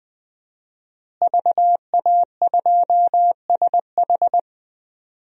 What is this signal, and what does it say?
Morse code, heard as a keyed tone, about 20 wpm, 700 Hz. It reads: VA2SH
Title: DXCC-100 Call Signs - 5 Characters - Repeated 3x - 20wpm